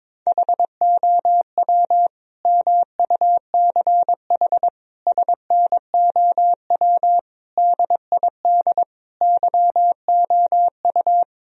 Morse code, heard as a keyed tone, about 22 wpm, 700 Hz. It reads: HOW MUCH SNOW DID YOU
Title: Sentences from Top 1000 Words - Review Part 3 - 22wpm